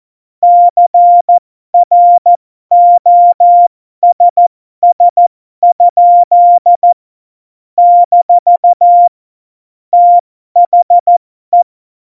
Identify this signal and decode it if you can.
Morse code, heard as a keyed tone, about 14 wpm, spaced as for 12 wpm, 700 Hz. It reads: CROSS? - THE
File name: Morsekurs / Witze